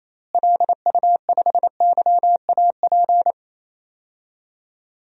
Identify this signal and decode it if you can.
Morse code, heard as a keyed tone, about 28 wpm, 700 Hz. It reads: LU5YAP
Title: DXCC-100 Call Signs - 6 Characters - Repeated 3x - 28wpm